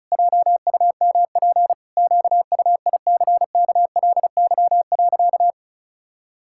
Morse code, heard as a keyed tone, about 34 wpm, 700 Hz. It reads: JUMP QUICKLY.
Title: Pangram Challenge v2 - 35wpm